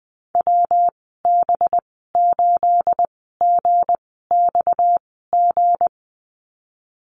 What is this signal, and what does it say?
Morse code, heard as a keyed tone, about 20 wpm, spaced as for 15 wpm, 700 Hz. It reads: WB8GXG
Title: 238 Lesson 38g - Letter q - Call Signs 20wpm